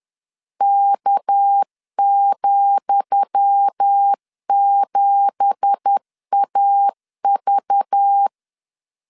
Morse code, heard as a keyed tone, about 10 wpm, 800 Hz. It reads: K,7AV
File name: LEZIONI / 10 SIGNS / 10D COMA 10wpm 3013